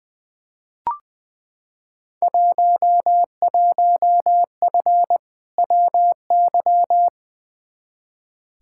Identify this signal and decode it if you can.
Morse code, heard as a keyed tone, about 20 wpm, 700 Hz. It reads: 11F WY
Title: Field Day Exchanges Encore - 20wpm